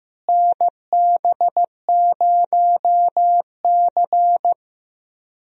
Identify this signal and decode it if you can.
Morse code, heard as a keyed tone, about 15 wpm, 700 Hz. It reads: NB0C
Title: USA Call Signs - 4 Characters - Repeated 3x - 15wpm